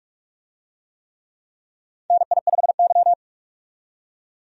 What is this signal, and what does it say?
Morse code, heard as a keyed tone, about 45 wpm, 700 Hz. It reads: NI5Y
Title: USA Call Signs - 4 Characters - 45wpm